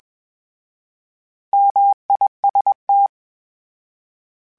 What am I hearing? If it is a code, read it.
Morse code, heard as a keyed tone, about 21 wpm, 800 Hz. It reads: MIST
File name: Misc / Misc-mp3s / Misc @20